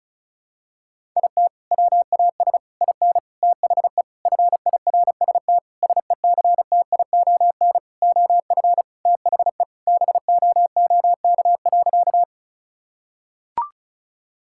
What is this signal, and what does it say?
Morse code, heard as a keyed tone, about 36 wpm, 700 Hz. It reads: IT WAS IN THE FIRST SECTION OF THE BOOK.
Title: Sentences from Top 1000 Words - Review Part 5 - 35wpm